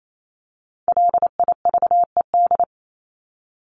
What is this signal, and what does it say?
Morse code, heard as a keyed tone, about 28 wpm, 700 Hz. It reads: LIVED
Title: ED Suffix - Top 100 Words with Base Words 28wpm